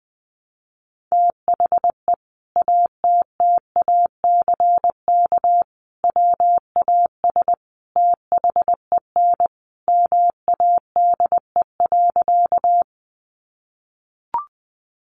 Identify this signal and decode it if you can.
Morse code, heard as a keyed tone, about 20 wpm, 700 Hz. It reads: THE ATTACK WAS THEN MADE.
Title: Sentences from Top 1500 Words 20wpm